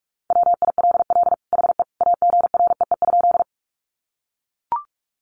Morse code, heard as a keyed tone, about 44 wpm, 700 Hz. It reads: WILL HE AGREE?
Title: Sentences from Top 1100 Words 45wpm